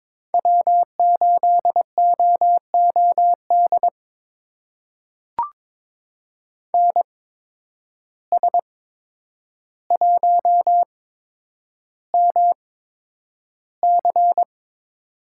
Morse code, heard as a keyed tone, about 22 wpm, spaced as for 9 wpm, 700 Hz. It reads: W8OOD NS1MC
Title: USA Call Signs - 5 Characters - 8x Character Spacing - 22wpm